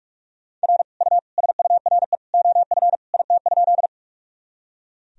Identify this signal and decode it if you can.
Morse code, heard as a keyed tone, about 46 wpm, 700 Hz. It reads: R U SURE OF IT?
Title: Sentences from Top 300 Words and Abbreviations 45wpm